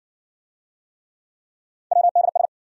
Morse code, heard as a keyed tone, about 59 wpm, 700 Hz. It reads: WDS